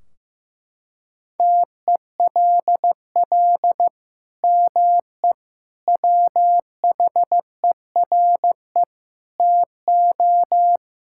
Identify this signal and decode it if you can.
Morse code, heard as a keyed tone, about 16 wpm, 700 Hz. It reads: TELL ME WHERE TO
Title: Sentences from Top 1600 Words 15wpm